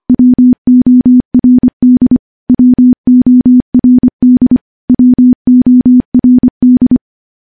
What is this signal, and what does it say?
Morse code, heard as a keyed tone, about 25 wpm, 250 Hz. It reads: WORD WORD WORD
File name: morse / morse-training-25WPM